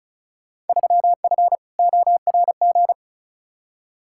Morse code, heard as a keyed tone, about 36 wpm, 700 Hz. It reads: 3F ORG
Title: Field Day Exchanges - 35wpm